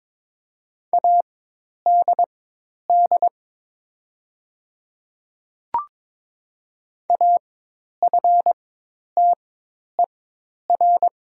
Morse code, heard as a keyed tone, about 22 wpm, spaced as for 10 wpm, 700 Hz. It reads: ADD AFTER
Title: 4x Character Spacing - Top 200 Words - 22wpm